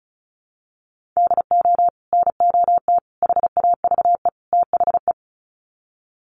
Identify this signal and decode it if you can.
Morse code, heard as a keyed tone, about 34 wpm, 700 Hz. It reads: DO NOT HAVE THE
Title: Sets of 4 Words 35wpm